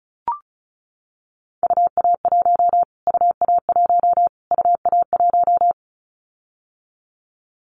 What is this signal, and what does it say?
Morse code, heard as a keyed tone, about 34 wpm, 700 Hz. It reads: UA1 UA1 UA1
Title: Callsign Prefixes for Top 25 DXCC - Repeated 3x - 35wpm